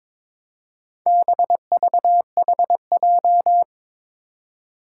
Morse code, heard as a keyed tone, about 22 wpm, 700 Hz. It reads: BVHJ
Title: Four Challenging Letter-Number Combinations 22wpm